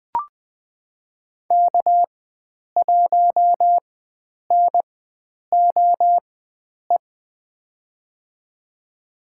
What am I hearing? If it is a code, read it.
Morse code, heard as a keyed tone, about 20 wpm, spaced as for 9 wpm, 700 Hz. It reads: K1NOE
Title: USA Call Signs - 4x Character Spacing - 20wpm